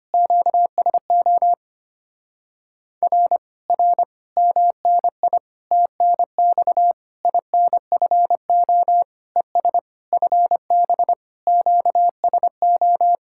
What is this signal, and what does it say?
Morse code, heard as a keyed tone, about 25 wpm, 700 Hz. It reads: QSO R R MNI TNX INFO ES FB QSO
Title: Speed-Racing Common QSO Phrases - 25wpm